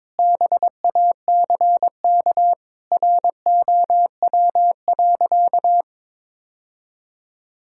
Morse code, heard as a keyed tone, about 22 wpm, 700 Hz. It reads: BACK ROW.
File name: practice / 22wpm / Sentences with 3-Word Phrases - 22wpm